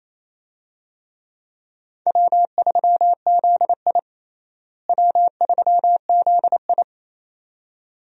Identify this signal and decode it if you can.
Morse code, heard as a keyed tone, about 28 wpm, 700 Hz. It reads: W3ZI W3ZI
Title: USA Call Signs - 4 Characters - Repeated 2x - 28wpm